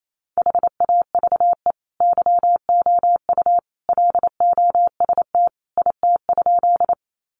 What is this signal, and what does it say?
Morse code, heard as a keyed tone, about 28 wpm, 700 Hz. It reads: HAVE YOU LOST IT?
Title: Sentences from Top 1400 Words 28wpm